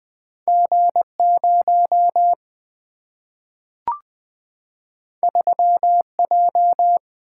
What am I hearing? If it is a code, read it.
Morse code, heard as a keyed tone, about 20 wpm, 700 Hz. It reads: G 0 3 J
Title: Single-Letter Single-Number 20wpm